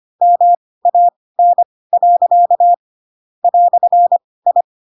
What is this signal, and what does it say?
Morse code, heard as a keyed tone, about 25 wpm, spaced as for 18 wpm, 700 Hz. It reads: MAN. "I
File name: Pinocchio ch3 12 25 ik1hge